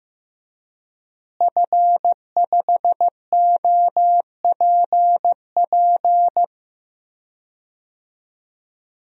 Morse code, heard as a keyed tone, about 16 wpm, 700 Hz. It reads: F5OPP
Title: DXCC-100 Call Signs - 5 Characters - Repeated 2x - 15wpm